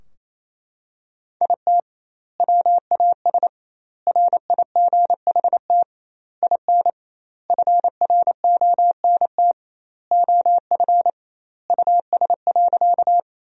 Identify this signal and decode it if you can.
Morse code, heard as a keyed tone, about 28 wpm, 700 Hz. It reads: IT WAS RIGHT IN FRONT OF US.
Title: 2x Word-Spacing - Sentences from Top 500 Words - 28wpm